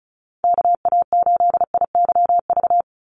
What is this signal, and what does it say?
Morse code, heard as a keyed tone, about 34 wpm, 700 Hz. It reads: KA8IYV
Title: USA Call Signs 35wpm